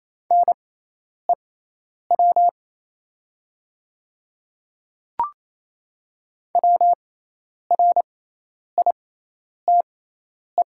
Morse code, heard as a keyed tone, about 28 wpm, spaced as for 10 wpm, 700 Hz. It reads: NEW WRITE
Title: 6x Character Spacing - Top 300 Words - 28wpm